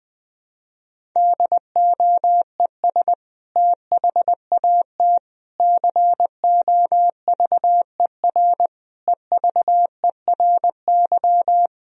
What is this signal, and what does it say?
Morse code, heard as a keyed tone, about 21 wpm, 700 Hz. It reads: DOES THAT COVER EVERY
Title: Sentences from Top 1000 Words - Review Part 1 - 20wpm